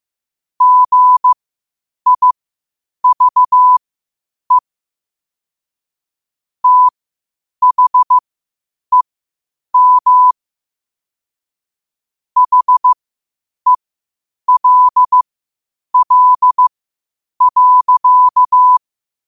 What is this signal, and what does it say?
Morse code, heard as a keyed tone, about 15 wpm, spaced as for 8 wpm, 1000 Hz. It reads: GIVE THEM HELL.
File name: qotdfiles / 8wpm